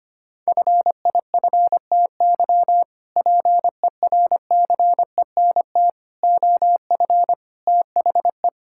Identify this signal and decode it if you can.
Morse code, heard as a keyed tone, about 25 wpm, 700 Hz. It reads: FIFTY PERCENT OF THE
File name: practice / 25wpm / Sentences with 3-Word Phrases - 25wpm